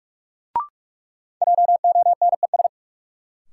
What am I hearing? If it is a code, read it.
Morse code, heard as a keyed tone, about 45 wpm, 700 Hz. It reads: JONES
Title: Sets of 5 Words - New Word Drill - 45wpm